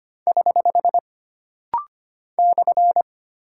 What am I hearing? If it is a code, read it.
Morse code, heard as a keyed tone, about 25 wpm, 700 Hz. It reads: <HH> /
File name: Single Punctuation-Prosign - Rapid-Fire - 25wpm